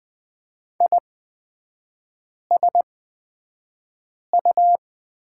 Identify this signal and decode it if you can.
Morse code, heard as a keyed tone, about 20 wpm, spaced as for 5 wpm, 700 Hz. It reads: ISU
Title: 236 Lesson 38e - Letter q - Three Characters 20wpm